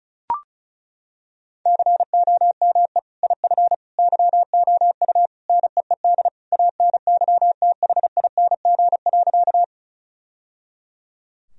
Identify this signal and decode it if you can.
Morse code, heard as a keyed tone, about 34 wpm, 700 Hz. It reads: COME IF YOU NEED ANYTHING.
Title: Sentences from Top 1000 Words - Review Part 1 - 35wpm